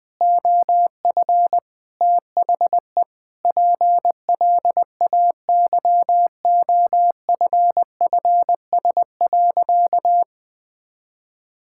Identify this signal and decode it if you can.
Morse code, heard as a keyed tone, about 20 wpm, 700 Hz. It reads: OF THE PLAYOFFS.
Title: Sentences with 7-Word Phrases - 20wpm